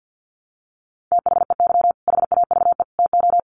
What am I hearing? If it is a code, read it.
Morse code, heard as a keyed tone, about 50 wpm, 700 Hz. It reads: THEY HAVE TO
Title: Sets of 3 Words 50wpm